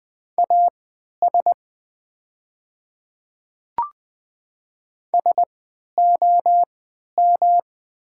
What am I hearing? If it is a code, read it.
Morse code, heard as a keyed tone, about 20 wpm, spaced as for 11 wpm, 700 Hz. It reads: AS SOM
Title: 3x Character Spacing - Top 100 Words - 20wpm